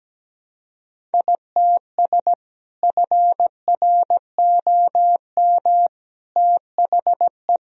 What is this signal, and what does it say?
Morse code, heard as a keyed tone, about 17 wpm, 700 Hz. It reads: ITS FROM THE
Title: Sentences from Top 1000 Words - Review Part 2 - 17wpm